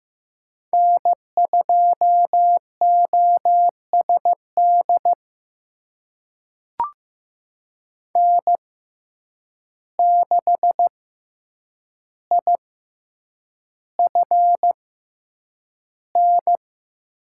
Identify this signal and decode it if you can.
Morse code, heard as a keyed tone, about 15 wpm, spaced as for 8 wpm, 700 Hz. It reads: N2OSD N6IFN
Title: USA Call Signs - 6x Character Spacing - 15wpm